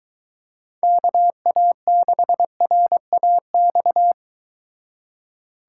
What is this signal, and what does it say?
Morse code, heard as a keyed tone, about 23 wpm, 700 Hz. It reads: KA6RAX